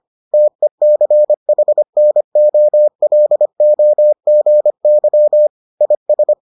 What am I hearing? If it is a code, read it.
Morse code, heard as a keyed tone, about 25 wpm, 600 Hz. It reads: TECHNOLOGY IS